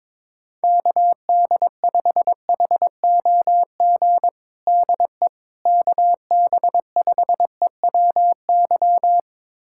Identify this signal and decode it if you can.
Morse code, heard as a keyed tone, about 23 wpm, 700 Hz. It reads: KD5HOG DE KB5EWY
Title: USA Callsign DE Callsign - 6 Characters - 22wpm